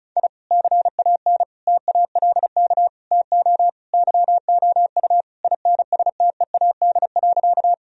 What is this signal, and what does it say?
Morse code, heard as a keyed tone, about 34 wpm, 700 Hz. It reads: I CAN TALK TO YOU INSTEAD.
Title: Sentences from Top 1000 Words - Review Part 4 - 35wpm